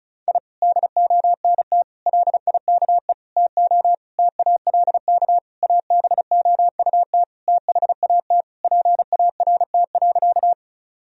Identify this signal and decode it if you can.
Morse code, heard as a keyed tone, about 36 wpm, 700 Hz. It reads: I DONT LIKE TO TALK ABOUT THAT PART.
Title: Sentences from Top 1000 Words - Review Part 1 - 35wpm